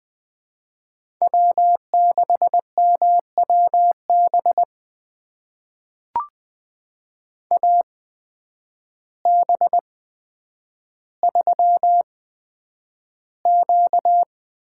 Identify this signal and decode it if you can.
Morse code, heard as a keyed tone, about 20 wpm, spaced as for 9 wpm, 700 Hz. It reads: W6MWB AB3Q
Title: USA Call Signs - 5 Characters - 8x Character Spacing - 20wpm